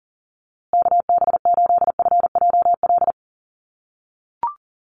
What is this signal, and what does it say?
Morse code, heard as a keyed tone, about 40 wpm, 700 Hz. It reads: KB8FJL
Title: USA Call Signs Repeated 3x 40wpm